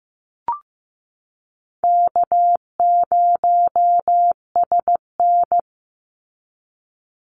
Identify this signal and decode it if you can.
Morse code, heard as a keyed tone, about 15 wpm, 700 Hz. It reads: K0SN
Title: Call Signs Repeated 3x - 15wpm